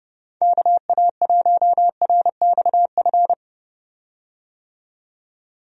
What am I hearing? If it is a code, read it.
Morse code, heard as a keyed tone, about 30 wpm, 700 Hz. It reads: KA1RXF
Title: USA Call Signs - 6 Characters - 30wpm